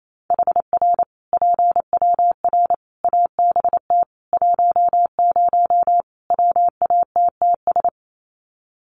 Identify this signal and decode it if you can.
Morse code, heard as a keyed tone, about 28 wpm, 700 Hz. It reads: HR PWR ABT 10 WATTS
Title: Common QSO Phrases 28wpm